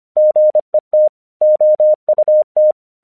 Morse code, heard as a keyed tone, about 26 wpm, 610 Hz. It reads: GET OUT
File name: 1 English 25wpm / AQC34